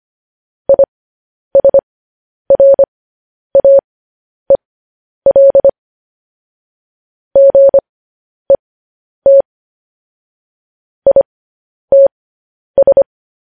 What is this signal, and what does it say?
Morse code, heard as a keyed tone, about 25 wpm, spaced as for 10 wpm, 550 Hz. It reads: ISRAEL GET ITS
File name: podcast / News headlines-1x-25x10